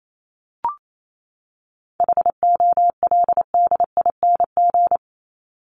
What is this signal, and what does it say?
Morse code, heard as a keyed tone, about 28 wpm, 700 Hz. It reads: HOLDING
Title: ING Suffix - Top 100 Words with Base Words 28wpm